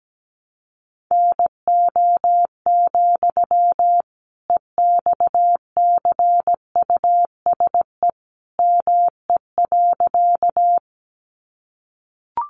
Morse code, heard as a keyed tone, about 17 wpm, 700 Hz. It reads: NO, EXCUSE ME.
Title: Sentences from Top 700 Words 17wpm